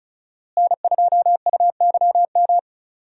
Morse code, heard as a keyed tone, about 34 wpm, 700 Hz. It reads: N2UYM
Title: USA Call Signs - 5 Characters - Repeated 2x - 35wpm